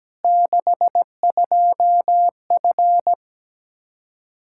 Morse code, heard as a keyed tone, about 17 wpm, 700 Hz. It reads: 62F
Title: Practice QSO - Exchange 2 - Info - 17wpm